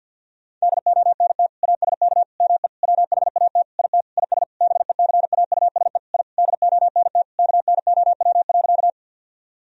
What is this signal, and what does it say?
Morse code, heard as a keyed tone, about 49 wpm, 700 Hz. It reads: DONT ASK ME WHAT IT IS BECAUSE I DONT KNOW.
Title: Sentences from Top 1000 Words - Review Part 3 - 50wpm